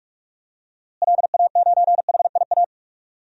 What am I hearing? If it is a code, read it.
Morse code, heard as a keyed tone, about 43 wpm, 700 Hz. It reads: LA9HIA
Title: DXCC-100 Call Signs - 6 Characters - Repeated 3x - 45wpm